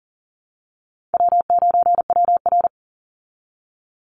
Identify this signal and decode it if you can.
Morse code, heard as a keyed tone, about 40 wpm, 700 Hz. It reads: W9WR
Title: Easy Call Signs Repeated 2x - 40wpm